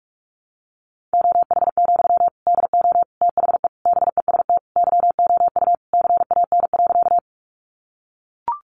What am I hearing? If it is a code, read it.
Morse code, heard as a keyed tone, about 44 wpm, 700 Hz. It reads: OH, DO THE BEST YOU CAN.
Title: Sentences from Top 500 Words 45wpm